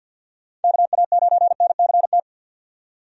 Morse code, heard as a keyed tone, about 51 wpm, 700 Hz. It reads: KA9NCT
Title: USA Call Signs - 6 Characters - 50wpm